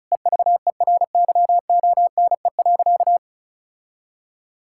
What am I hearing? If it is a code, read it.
Morse code, heard as a keyed tone, about 36 wpm, 700 Hz. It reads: EVERYONE.
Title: Sentences from Top 1000 Words - Review Part 5 - 35wpm